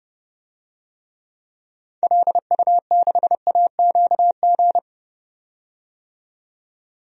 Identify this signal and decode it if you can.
Morse code, heard as a keyed tone, about 30 wpm, 700 Hz. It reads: LU6AQG
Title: DXCC-100 Call Signs - 6 Characters - 30wpm